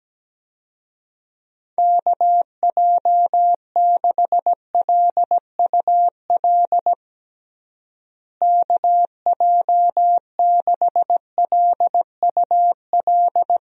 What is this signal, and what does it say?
Morse code, heard as a keyed tone, about 17 wpm, 700 Hz. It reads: KJ6LUL KJ6LUL
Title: USA Call Signs - 6 Characters - Repeated 2x - 17wpm